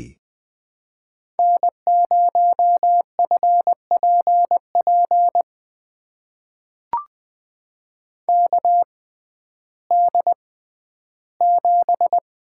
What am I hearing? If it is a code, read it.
Morse code, heard as a keyed tone, about 21 wpm, spaced as for 12 wpm, 700 Hz. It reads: N0FPP KD7
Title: USA Call Signs - 5 Characters - 6x Character Spacing - 20wpm